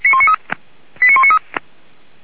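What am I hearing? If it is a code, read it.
Five-tone selective call (ZVEI), heard as quick runs of five tones, tones of about 70 ms. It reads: 89193
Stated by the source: Received on SDR.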